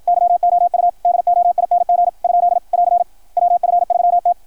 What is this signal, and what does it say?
Morse code, heard as a keyed tone, about 53 wpm, 660 Hz, with radio noise. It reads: YOU DOING ? <AR> WF3T
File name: archived / CRSnet